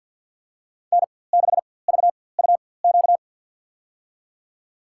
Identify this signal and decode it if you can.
Morse code, heard as a keyed tone, about 51 wpm, spaced as for 23 wpm, 700 Hz. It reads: N6VUQ
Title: USA Call Signs - 5 Characters - 4x Character Spacing - 50wpm